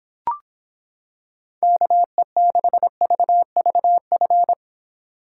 Morse code, heard as a keyed tone, about 26 wpm, 700 Hz. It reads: KE6VVF